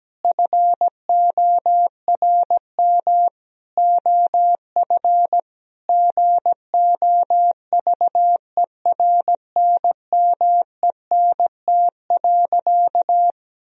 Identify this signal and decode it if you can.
Morse code, heard as a keyed tone, about 17 wpm, 700 Hz. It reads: FORM OF GOVERNMENT.
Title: Sentences from Top 1000 Words - Review Part 3 - 17wpm